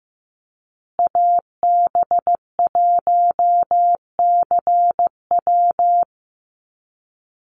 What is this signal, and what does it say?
Morse code, heard as a keyed tone, about 15 wpm, 700 Hz. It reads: AB1CW
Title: Call Signs Repeated 2x - 15wpm